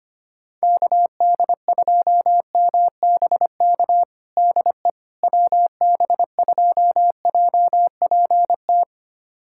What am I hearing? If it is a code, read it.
Morse code, heard as a keyed tone, about 25 wpm, 700 Hz. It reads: KD2MBK DE WB2JPT
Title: USA Callsign DE Callsign 25wpm